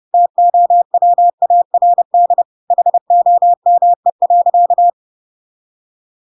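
Morse code, heard as a keyed tone, about 30 wpm, 700 Hz. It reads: TOWARD HOME.
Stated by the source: Chapter 2 of The Adventures of Pinocchio read as a book in Morse code (CW) at 30 wpm, by ik1hge.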